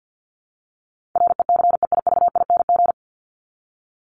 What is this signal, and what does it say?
Morse code, heard as a keyed tone, about 50 wpm, 700 Hz. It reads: RECEIVING
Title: ING Suffix - Top 100 Words with Base Words 50wpm